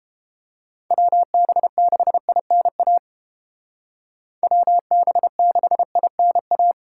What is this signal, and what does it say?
Morse code, heard as a keyed tone, about 28 wpm, 700 Hz. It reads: WB6INA WB6INA